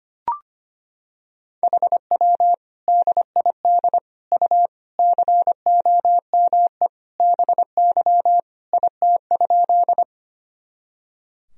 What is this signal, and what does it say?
Morse code, heard as a keyed tone, about 25 wpm, 700 Hz. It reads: HW DID U COME BY IT?
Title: Sentences from Top 100 Words and Abbreviations 25wpm